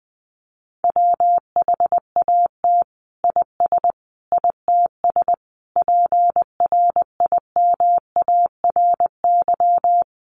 Morse code, heard as a keyed tone, about 20 wpm, 700 Hz. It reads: WHAT IS ITS PRIMARY
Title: Sentences from Top 1100 Words 20wpm